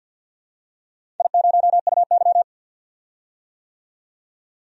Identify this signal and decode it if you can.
Morse code, heard as a keyed tone, about 48 wpm, 700 Hz. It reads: I0UY
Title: DXCC-100 Call Signs - 4 Characters - 50wpm